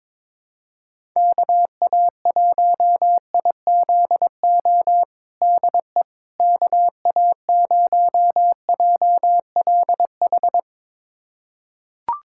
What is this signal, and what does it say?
Morse code, heard as a keyed tone, about 22 wpm, 700 Hz. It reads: KA1IZO DE KA0JLH
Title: USA Callsign DE Callsign - 6 Characters - 22wpm